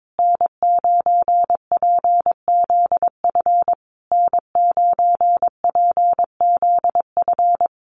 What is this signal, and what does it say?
Morse code, heard as a keyed tone, about 22 wpm, 700 Hz. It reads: N9PZF N9PZF
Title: USA Call Signs Repeated 3x 22wpm